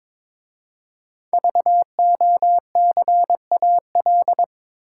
Morse code, heard as a keyed tone, about 22 wpm, 700 Hz. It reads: VOCAL
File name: 5 Letter Words - First 3 Letters Given - 22wpm